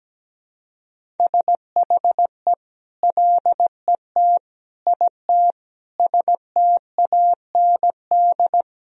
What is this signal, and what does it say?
Morse code, heard as a keyed tone, about 17 wpm, 700 Hz. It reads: SHE LET IT STAND
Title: Sentences from Top 1000 Words - Review Part 1 - 17wpm